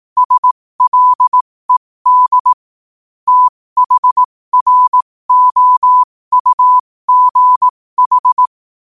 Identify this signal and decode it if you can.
Morse code, heard as a keyed tone, about 18 wpm, spaced as for 15 wpm, 1000 Hz. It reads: SLED THROUGH
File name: qotdfiles / 15wpm